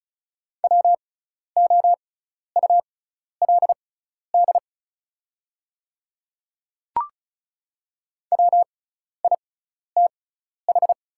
Morse code, heard as a keyed tone, about 36 wpm, spaced as for 12 wpm, 690 Hz. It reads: WOULD WITH
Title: 6x Character Spacing - Top 100 Words - 35wpm